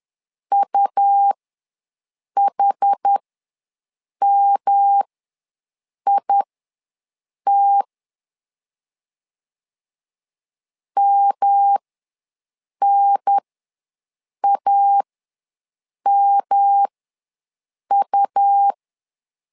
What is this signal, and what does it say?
Morse code, heard as a keyed tone, about 10 wpm, spaced as for 6 wpm, 800 Hz. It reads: UHMIT MNAMU
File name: LEZIONI / 3 MWRO / 3 M ws 10wpm 2121